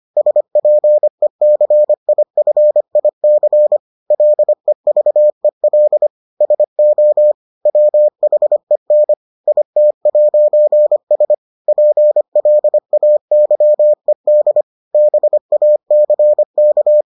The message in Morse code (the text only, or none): SPECIFIC LEVEL SO WHEN IT'S PLAYED BACK